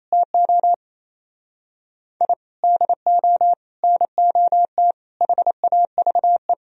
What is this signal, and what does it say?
Morse code, heard as a keyed tone, about 35 wpm, 700 Hz. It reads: TO I DO NOT HAVE